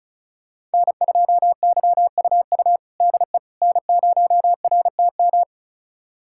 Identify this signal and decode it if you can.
Morse code, heard as a keyed tone, about 34 wpm, 700 Hz. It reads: N2YUU DE N0RTM
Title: USA Callsign DE Callsign - 5 Characters - 35wpm